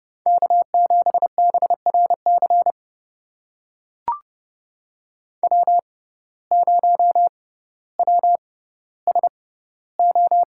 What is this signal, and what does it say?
Morse code, heard as a keyed tone, about 30 wpm, spaced as for 15 wpm, 700 Hz. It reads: K7BRC W0WSO
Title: USA Call Signs - 5 Characters - 6x Character Spacing - 30wpm